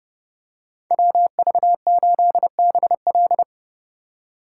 Morse code, heard as a keyed tone, about 30 wpm, 700 Hz. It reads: WV8BL
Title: USA Call Signs - 5 Characters - 2x Character Spacing - 30wpm